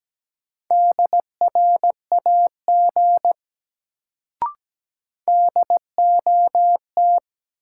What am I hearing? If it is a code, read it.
Morse code, heard as a keyed tone, about 17 wpm, 700 Hz. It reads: DRAG DOT
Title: Sentences from Top 1500 Words - New Word Drill - 17wpm